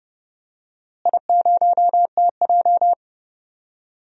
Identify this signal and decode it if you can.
Morse code, heard as a keyed tone, about 29 wpm, 700 Hz. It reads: I0TJ
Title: DXCC-100 Call Signs - 4 Characters - Repeated 3x - 30wpm